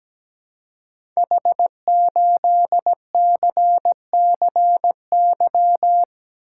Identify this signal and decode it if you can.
Morse code, heard as a keyed tone, about 17 wpm, 700 Hz. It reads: H8CCY
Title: Code Groups - 5 Letter:Numbers - 17wpm